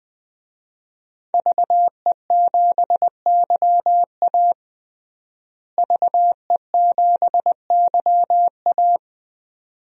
Morse code, heard as a keyed tone, about 21 wpm, 700 Hz. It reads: VE7YA VE7YA
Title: DXCC-100 Call Signs - 5 Characters - Repeated 2x - 20wpm